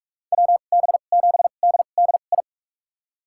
Morse code, heard as a keyed tone, about 46 wpm, spaced as for 33 wpm, 700 Hz. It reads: WB7DDI
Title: USA Call Signs - 6 Characters - 2x Character Spacing - 45wpm